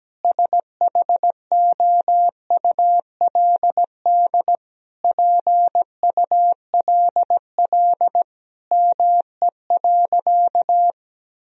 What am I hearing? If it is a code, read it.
Morse code, heard as a keyed tone, about 17 wpm, 700 Hz. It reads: SHOULD PULL ME.
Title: Sentences from Top 1000 Words - Review Part 2 - 17wpm